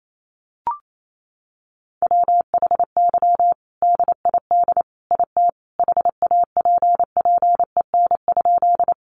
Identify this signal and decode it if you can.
Morse code, heard as a keyed tone, about 28 wpm, 700 Hz. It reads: WHY DID IT HAPPEN?
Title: Sentences from Top 400 Words 28wpm